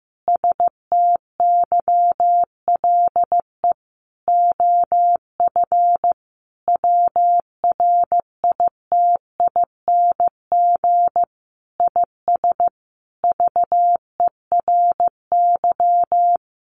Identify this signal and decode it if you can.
Morse code, heard as a keyed tone, about 15 wpm, 700 Hz. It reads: STYLE OF WRITING IS VERY
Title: Sentences from Top 800 Words 15wpm